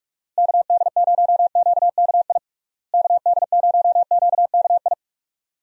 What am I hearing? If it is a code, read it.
Morse code, heard as a keyed tone, about 46 wpm, 700 Hz. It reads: KD0QKI KD0QKI
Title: USA Call Signs - 6 Characters - Repeated 2x - 45wpm